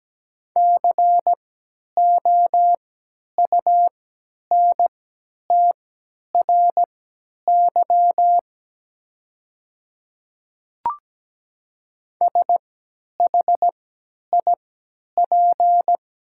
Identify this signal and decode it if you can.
Morse code, heard as a keyed tone, about 17 wpm, spaced as for 10 wpm, 690 Hz. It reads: COUNTRY SHIP
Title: 3x Character Spacing - Top 500 Words - 17wpm